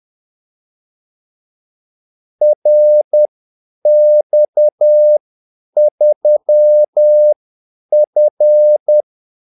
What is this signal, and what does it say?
Morse code, heard as a keyed tone, about 10 wpm, spaced as for 8 wpm, 600 Hz. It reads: RX3F
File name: Callsigns-10-8wpm